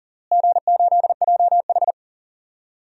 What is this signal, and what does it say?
Morse code, heard as a keyed tone, about 40 wpm, 700 Hz. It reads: G8JH
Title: DXCC-100 Call Signs - 4 Characters - Repeated 3x - 40wpm